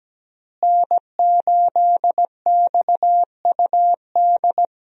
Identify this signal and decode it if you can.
Morse code, heard as a keyed tone, about 17 wpm, 700 Hz. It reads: N8XUD
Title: USA Call Signs - 5 Characters - Repeated 2x - 17wpm